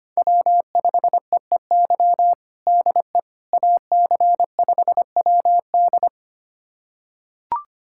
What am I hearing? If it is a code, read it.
Morse code, heard as a keyed tone, about 25 wpm, 700 Hz. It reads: W5EEY DE AC5WD
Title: USA Callsign DE Callsign - 5 Characters - 25wpm